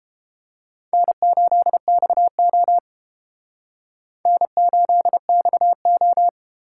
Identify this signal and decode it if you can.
Morse code, heard as a keyed tone, about 29 wpm, 700 Hz. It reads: N8XO N8XO